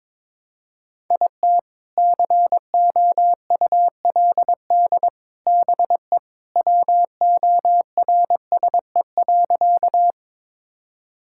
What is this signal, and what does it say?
Morse code, heard as a keyed tone, about 22 wpm, 700 Hz. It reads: IT COULD BE WORSE.
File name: practice / 22wpm / Sentences with 3-Word Phrases - 22wpm